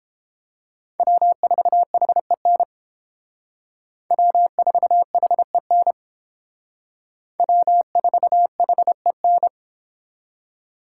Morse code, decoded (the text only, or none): W4HEN W4HEN W4HEN